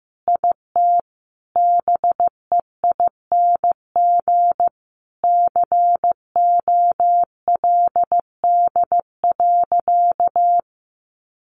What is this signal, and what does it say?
Morse code, heard as a keyed tone, about 15 wpm, 700 Hz. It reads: IT BEING COLD.
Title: Sentences from Top 800 Words 15wpm